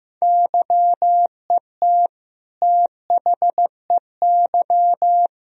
Morse code, heard as a keyed tone, about 15 wpm, 700 Hz. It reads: YET THEY
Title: Sentences from Top 1000 Words - Review Part 3 - 15wpm